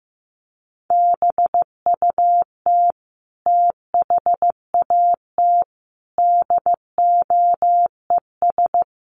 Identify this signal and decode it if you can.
Morse code, heard as a keyed tone, about 15 wpm, 700 Hz. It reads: BUT THAT DOES
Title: Sets of 5 Words 15wpm